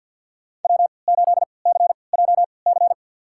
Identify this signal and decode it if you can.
Morse code, heard as a keyed tone, about 48 wpm, spaced as for 28 wpm, 700 Hz. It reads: W8CJC
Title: USA Call Signs - 5 Characters - 3x Character Spacing - 50wpm